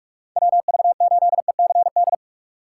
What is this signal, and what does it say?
Morse code, heard as a keyed tone, about 46 wpm, 700 Hz. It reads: WV8ECD
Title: USA Call Signs - 6 Characters - 6x Character Spacing - 45wpm